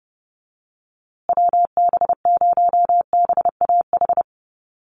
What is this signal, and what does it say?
Morse code, heard as a keyed tone, about 30 wpm, 700 Hz. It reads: WB0BAH
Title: USA Call Signs Repeated 3x 30wpm